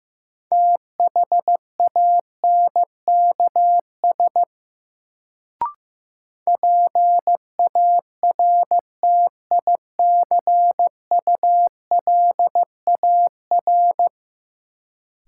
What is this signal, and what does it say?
Morse code, heard as a keyed tone, about 15 wpm, 700 Hz. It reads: THANKS PARTICULAR
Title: Sentences from Top 600 Words - New Word Drill - 15wpm